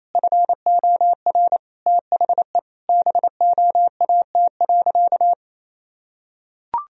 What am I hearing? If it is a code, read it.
Morse code, heard as a keyed tone, about 28 wpm, 700 Hz. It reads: FOR THE BOAT.
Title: Sentences from Top 1000 Words - Review Part 3 - 28wpm